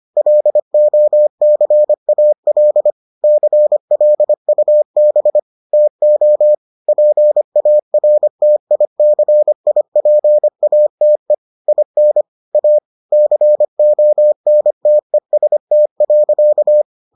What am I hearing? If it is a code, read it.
Morse code, heard as a keyed tone, about 25 wpm, 600 Hz. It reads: LOCAL CLUB TO PARTICIPATE IN A CONTEST.